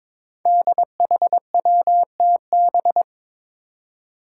Morse code, heard as a keyed tone, about 22 wpm, 700 Hz. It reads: DHWTB
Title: Five Challenging Letter-Number Combinations 22wpm